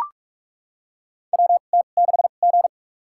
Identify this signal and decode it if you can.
Morse code, heard as a keyed tone, about 42 wpm, spaced as for 31 wpm, 700 Hz. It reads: WT6G
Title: USA Call Signs - 2x Character Spacing - 45wpm